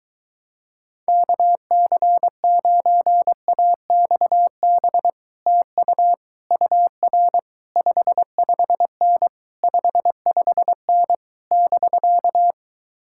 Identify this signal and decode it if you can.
Morse code, heard as a keyed tone, about 23 wpm, 700 Hz. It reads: KC9AXB TU UR 55N 55N <BK>